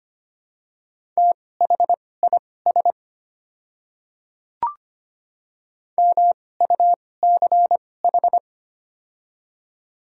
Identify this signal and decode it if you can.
Morse code, heard as a keyed tone, about 25 wpm, spaced as for 18 wpm, 700 Hz. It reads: THIS MUCH
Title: 2x Character Spacing - Top 300 Words - 25wpm